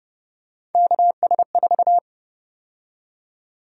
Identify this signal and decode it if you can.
Morse code, heard as a keyed tone, about 31 wpm, 700 Hz. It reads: KS4
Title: USA Call Sign Prefixes - 30wpm